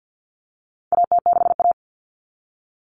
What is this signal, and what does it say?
Morse code, heard as a keyed tone, about 50 wpm, 700 Hz. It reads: AT6A
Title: Common Cut Numbers - 4-Digit 50wpm